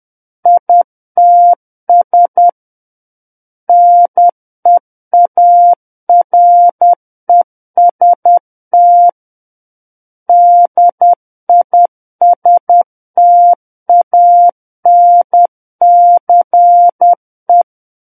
Morse code, heard as a keyed tone, about 10 wpm, 700 Hz. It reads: ITS NEAREST DISTANCE
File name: Book One / 01-Chapter 1 - 10 WPM